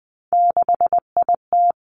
Morse code, heard as a keyed tone, about 20 wpm, 700 Hz. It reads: 6IT
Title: Three Challenging Letter-Number Combinations 20wpm